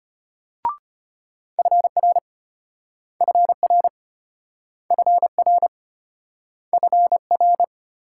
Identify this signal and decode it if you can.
Morse code, heard as a keyed tone, about 34 wpm, 700 Hz. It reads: FR FR FR FR